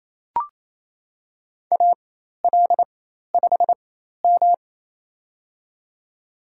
Morse code, heard as a keyed tone, about 27 wpm, spaced as for 13 wpm, 700 Hz. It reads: AL5M
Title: USA Call Signs - 4 Characters - 4x Character Spacing - 28wpm